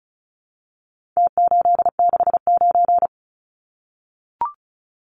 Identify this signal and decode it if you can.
Morse code, heard as a keyed tone, about 34 wpm, 700 Hz. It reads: T869
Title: Numbers - Contest Style - 4-Digit 35wpm